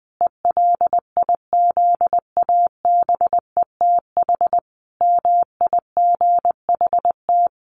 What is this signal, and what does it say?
Morse code, heard as a keyed tone, about 20 wpm, 700 Hz. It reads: ELIZABETH MIGHT
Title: A Sound Of Thunder 20wpm